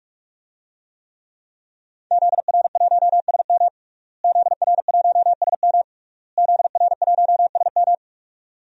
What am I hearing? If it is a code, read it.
Morse code, heard as a keyed tone, about 46 wpm, 700 Hz. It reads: ZR1SM ZR1SM ZR1SM
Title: DXCC-100 Call Signs - 5 Characters - Repeated 3x - 45wpm